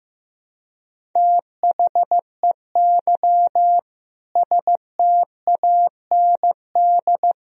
Morse code, heard as a keyed tone, about 15 wpm, 700 Hz. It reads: THEY STAND
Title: Sentences from Top 1000 Words - Review Part 1 - 15wpm